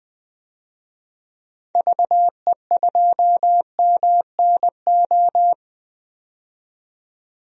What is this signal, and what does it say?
Morse code, heard as a keyed tone, about 20 wpm, 700 Hz. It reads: VE2MNO
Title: DXCC-100 Call Signs - 6 Characters - 20wpm